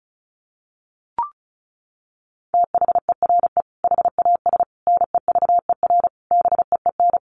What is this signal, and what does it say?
Morse code, heard as a keyed tone, about 34 wpm, 700 Hz. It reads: THERE HAS NEVER BEEN
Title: Sentences from Top 1500 Words 35wpm